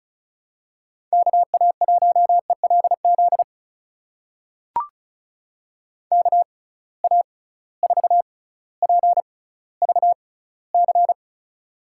Morse code, heard as a keyed tone, about 36 wpm, spaced as for 18 wpm, 700 Hz. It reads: KA1ELZ KA4PVC
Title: USA Call Signs - 6 Characters - 6x Character Spacing - 35wpm